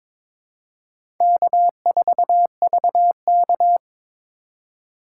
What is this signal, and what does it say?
Morse code, heard as a keyed tone, about 22 wpm, 700 Hz. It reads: K4VK
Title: USA Call Signs - 4 Characters - 4x Character Spacing - 22wpm